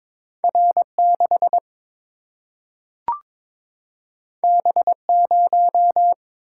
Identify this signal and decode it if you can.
Morse code, heard as a keyed tone, about 22 wpm, 700 Hz. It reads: R6 B0
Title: Single-Letter Single-Number 22wpm